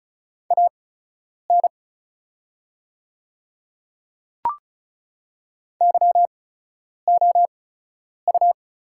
Morse code, heard as a keyed tone, about 36 wpm, spaced as for 10 wpm, 690 Hz. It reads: AN YOU
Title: 8x Character Spacing - Top 100 Words - 35wpm